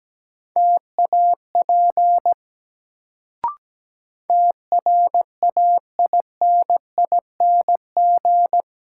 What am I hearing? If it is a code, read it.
Morse code, heard as a keyed tone, about 17 wpm, 700 Hz. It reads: TAP TRAINING
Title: Sentences from Top 1400 Words - New Word Drill Alt - 17wpm